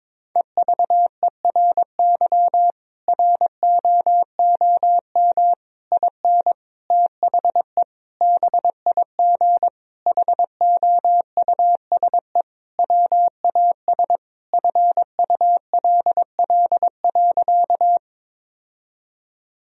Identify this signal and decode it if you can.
Morse code, heard as a keyed tone, about 22 wpm, 700 Hz. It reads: EVERY ROOM IN THE BIG HOUSE WAS FULL.
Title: Sentences from Top 1000 Words - Review Part 3 - 22wpm